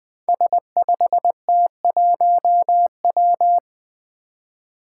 Morse code, heard as a keyed tone, about 20 wpm, 700 Hz. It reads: S5T1W
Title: Five Challenging Letter-Number Combinations 20wpm